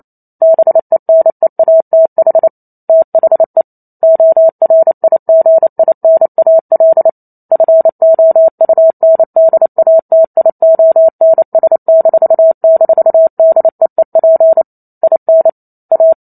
Morse code, heard as a keyed tone, about 28 wpm, 650 Hz. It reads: BENEATH THE ORIGINAL FOUNDATIONS--DEEP IN A